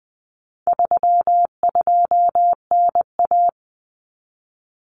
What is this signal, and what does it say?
Morse code, heard as a keyed tone, about 20 wpm, 700 Hz. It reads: 32NA
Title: Common Cut Numbers - 4-Digit 20wpm